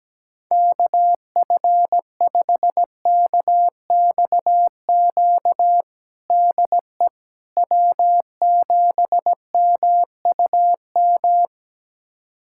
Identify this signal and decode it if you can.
Morse code, heard as a keyed tone, about 17 wpm, 700 Hz. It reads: KF5KXQ DE W7MUM
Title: USA Callsign DE Callsign 17wpm